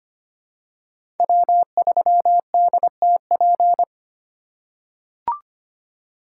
Morse code, heard as a keyed tone, about 25 wpm, 700 Hz. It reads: W3DTP
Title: USA Call Signs - 5 Characters - 4x Character Spacing - 25wpm